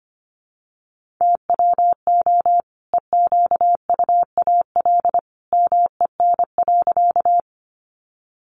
Morse code, heard as a keyed tone, about 25 wpm, 700 Hz. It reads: TWO EQUAL MEN.
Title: Sentences from Top 1400 Words 25wpm